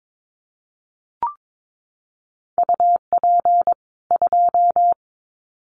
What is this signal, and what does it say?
Morse code, heard as a keyed tone, about 22 wpm, 700 Hz. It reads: UP 2
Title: Common QSO Phrases 22wpm